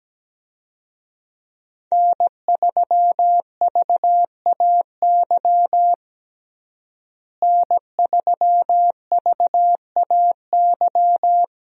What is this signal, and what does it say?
Morse code, heard as a keyed tone, about 17 wpm, 700 Hz. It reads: N3VAY N3VAY
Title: USA Call Signs - 5 Characters - Repeated 2x - 17wpm